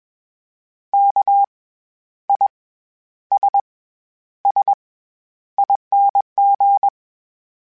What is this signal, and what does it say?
Morse code, heard as a keyed tone, about 21 wpm, spaced as for 11 wpm, 800 Hz. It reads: KISSING
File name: Special INGs-mp3s / Special INGs-mp3s / SlowToFaster ing @20